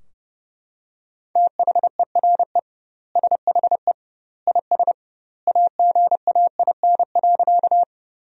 Morse code, heard as a keyed tone, about 30 wpm, 700 Hz. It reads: THERE SHE IS AGAIN.
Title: 2x Word-Spacing - Sentences from Top 200 Words - 30wpm